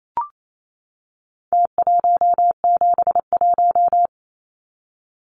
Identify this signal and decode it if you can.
Morse code, heard as a keyed tone, about 28 wpm, 700 Hz. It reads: T171
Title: Numbers - Contest Style - 4-Digit 28wpm